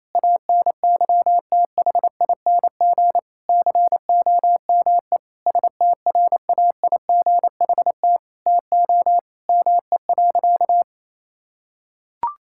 Morse code, heard as a keyed tone, about 28 wpm, 700 Hz. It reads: ANYTHING COME STRAIGHT TO ME.
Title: Sentences from Top 1000 Words - Review Part 6 - 28wpm